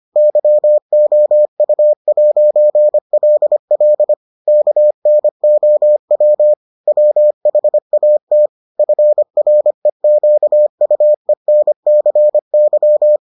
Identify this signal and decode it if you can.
Morse code, heard as a keyed tone, about 25 wpm, 600 Hz. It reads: YOU'LL KNOW WHAT FREQUENCY